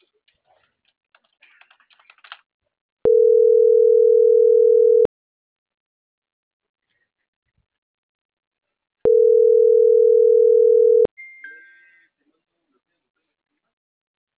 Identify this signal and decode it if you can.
Ringback tone (North America), 440 + 480 Hz.